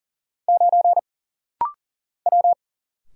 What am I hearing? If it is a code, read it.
Morse code, heard as a keyed tone, about 39 wpm, 700 Hz. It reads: 9 W
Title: Single Letter-Number - Rapid-Fire 40wpm